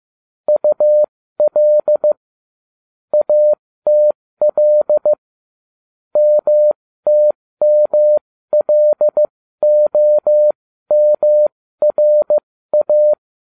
Morse code, heard as a keyed tone, about 15 wpm, spaced as for 13 wpm, 600 Hz. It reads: UL ATL MTMLOMRA